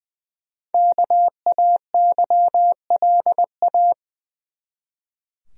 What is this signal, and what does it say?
Morse code, heard as a keyed tone, about 21 wpm, 700 Hz. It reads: KAYLA
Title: USA Common Names - Max 5 Letters - 20wpm